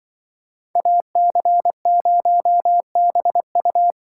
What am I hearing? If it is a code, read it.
Morse code, heard as a keyed tone, about 25 wpm, 700 Hz. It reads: AC0BU
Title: Speed Racing - USA 2x2 Call Signs 20wpm